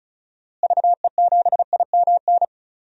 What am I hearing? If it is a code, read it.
Morse code, heard as a keyed tone, about 34 wpm, 700 Hz. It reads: VE7IMN
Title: DXCC-100 Call Signs - 6 Characters - Repeated 2x - 35wpm